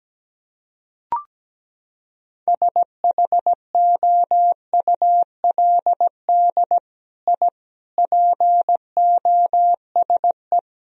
Morse code, heard as a keyed tone, about 17 wpm, 700 Hz. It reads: SHOULD I POSE
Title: Sentences from Top 1000 Words - Review Part 2 - 17wpm